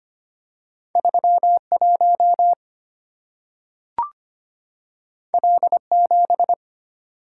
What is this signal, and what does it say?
Morse code, heard as a keyed tone, about 25 wpm, 690 Hz. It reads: 31 L7
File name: Two Letter-Number Combinations - 25wpm